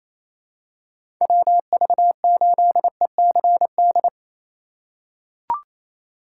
Morse code, heard as a keyed tone, about 28 wpm, 700 Hz. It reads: WV8ECD
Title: USA Call Signs - 6 Characters - 4x Character Spacing - 28wpm